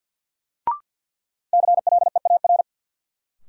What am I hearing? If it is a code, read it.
Morse code, heard as a keyed tone, about 52 wpm, 700 Hz. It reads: CLEAR
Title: Sets of 5 Words Encore - New Word Drill - 50wpm